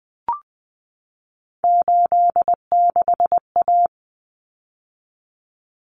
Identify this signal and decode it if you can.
Morse code, heard as a keyed tone, about 20 wpm, 700 Hz. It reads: 86A
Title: Common Cut Numbers - 3-Digit 20wpm